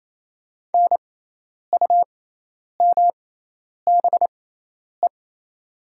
Morse code, heard as a keyed tone, about 28 wpm, spaced as for 10 wpm, 700 Hz. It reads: NUMBE
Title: 6x Character Spacing - Top 200 Words - 28wpm